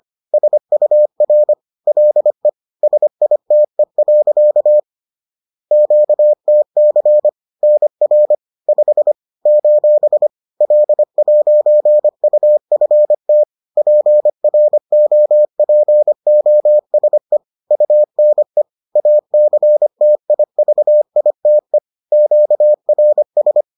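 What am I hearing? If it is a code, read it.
Morse code, heard as a keyed tone, about 25 wpm, 600 Hz. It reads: SUR LE SITE. QTC NR 5 : L'UFT PROPOSE UNE ACTIVITE QRS
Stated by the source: Bulletin sent in CW.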